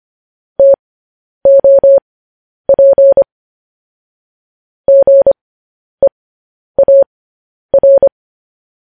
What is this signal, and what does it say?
Morse code, heard as a keyed tone, about 25 wpm, spaced as for 10 wpm, 550 Hz. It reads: TOP GEAR
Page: news headlines in morse code at 10 wpm